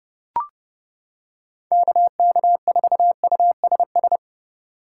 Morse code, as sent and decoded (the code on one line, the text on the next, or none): -.- -.- ....- ..- ... ...
KK4USS